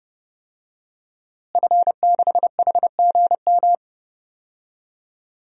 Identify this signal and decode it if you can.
Morse code, heard as a keyed tone, about 30 wpm, 700 Hz. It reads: F6HGM
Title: DXCC-100 Call Signs - 5 Characters - Repeated 2x - 30wpm